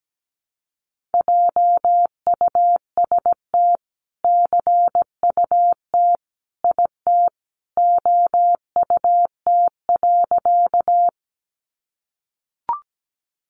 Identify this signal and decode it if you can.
Morse code, heard as a keyed tone, about 17 wpm, 700 Hz. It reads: JUST CUT IT OUT.
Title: Sentences from Top 300 Words 17wpm